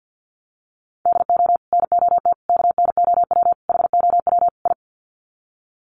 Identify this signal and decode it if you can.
Morse code, heard as a keyed tone, about 48 wpm, 700 Hz. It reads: DO NOT KNOW HOW I
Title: Sets of 5 Words Encore2 50wpm